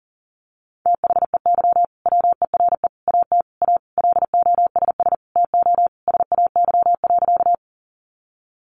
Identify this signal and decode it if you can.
Morse code, heard as a keyed tone, about 40 wpm, 700 Hz. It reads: THEY WERE AT A LOSS TO SAY.
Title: Sentences from Top 1400 Words 40wpm